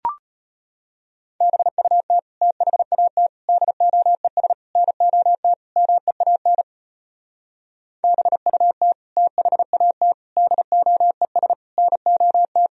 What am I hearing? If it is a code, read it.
Morse code, heard as a keyed tone, about 33 wpm, 700 Hz. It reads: BUT THAT DOES NOT MEAN BUT THAT DOES NOT